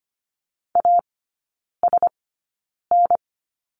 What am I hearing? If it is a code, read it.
Morse code, heard as a keyed tone, about 25 wpm, spaced as for 9 wpm, 700 Hz. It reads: ASN
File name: Lesson 15e - Number 5 - Three Characters 25wpm